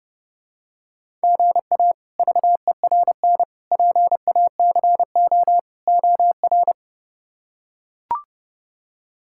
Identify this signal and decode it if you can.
Morse code, heard as a keyed tone, about 32 wpm, 700 Hz. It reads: GA VERN PACO OR
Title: K1USN SST QSO Practice - 30wpm